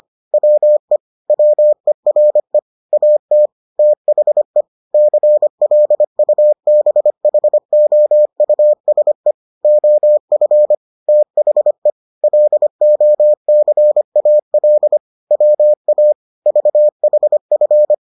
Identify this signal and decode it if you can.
Morse code, heard as a keyed tone, about 25 wpm, 600 Hz. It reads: WE WERE AT THE CLUBHOUSE OF THE LOCAL WA VHF